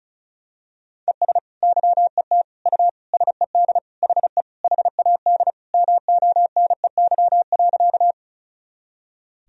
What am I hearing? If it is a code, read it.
Morse code, heard as a keyed tone, about 36 wpm, 700 Hz. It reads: ES YET U SED HE HAD MONEY.
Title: Sentences from Top 500 Words and Abbreviations 35wpm